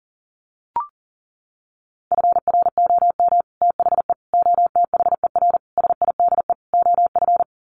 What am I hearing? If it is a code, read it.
Morse code, heard as a keyed tone, about 40 wpm, 700 Hz. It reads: FROM THE OTHER SIDE OF
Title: Sets of 5 Words Encore 40wpm